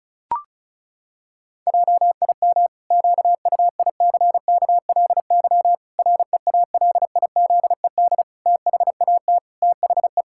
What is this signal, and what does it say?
Morse code, heard as a keyed tone, about 36 wpm, 700 Hz. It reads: JIM QUICKLY REALIZED THAT THE
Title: Pangram Challenge v2 - 35wpm